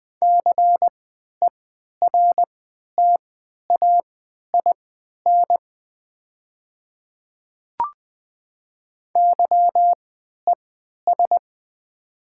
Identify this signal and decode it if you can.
Morse code, heard as a keyed tone, about 20 wpm, spaced as for 11 wpm, 700 Hz. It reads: CERTAIN YES
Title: 3x Character Spacing - Top 500 Words - 20wpm